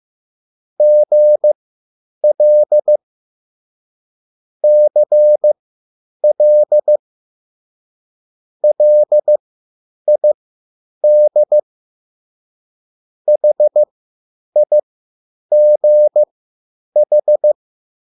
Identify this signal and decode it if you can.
Morse code, heard as a keyed tone, about 15 wpm, spaced as for 9 wpm, 600 Hz. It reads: GL CL LID HIGH
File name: QSL / CW lesson